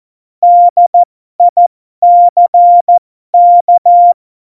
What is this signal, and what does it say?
Morse code, heard as a keyed tone, about 14 wpm, spaced as for 12 wpm, 700 Hz. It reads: DICK
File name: Morsekurs / Witze